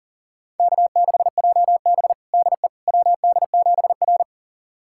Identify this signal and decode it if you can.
Morse code, heard as a keyed tone, about 43 wpm, 690 Hz. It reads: K6JB DE WD7R
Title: USA Callsign DE Callsign - 4 Characters - 40wpm